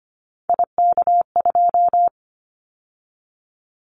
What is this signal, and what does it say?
Morse code, heard as a keyed tone, about 25 wpm, 700 Hz. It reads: IK2
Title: Callsign Prefixes for Top 25 DXCC - 25wpm